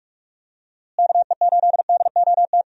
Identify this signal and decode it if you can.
Morse code, heard as a keyed tone, about 45 wpm, 700 Hz. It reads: KE8DOT
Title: USA Call Signs - 6 Characters - Repeated 3x - 45wpm